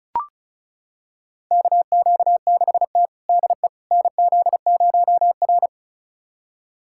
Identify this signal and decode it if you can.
Morse code, heard as a keyed tone, about 34 wpm, 700 Hz. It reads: KQ6T DE NZ0R
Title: USA Callsign DE Callsign - 4 Characters - 35wpm